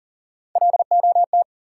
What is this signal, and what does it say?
Morse code, heard as a keyed tone, about 40 wpm, 700 Hz. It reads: LOT
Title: Sentences from Top 500 Words - New Word Drill - 40wpm